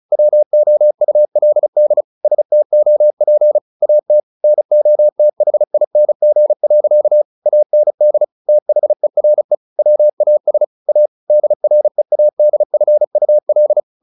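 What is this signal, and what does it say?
Morse code, heard as a keyed tone, about 34 wpm, 600 Hz. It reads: WOULD STOP AT NOTHING. AND THERE WAS A DREADFUL
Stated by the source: Morse practice at 35 wpm from CW med Gnister, sending Ken Follett's novel Never in English.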